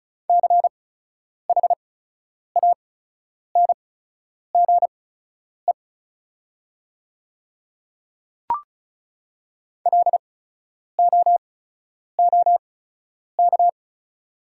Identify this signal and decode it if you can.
Morse code, heard as a keyed tone, about 36 wpm, spaced as for 10 wpm, 700 Hz. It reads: CHANGE LOOK
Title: 8x Character Spacing - Top 300 Words - 35wpm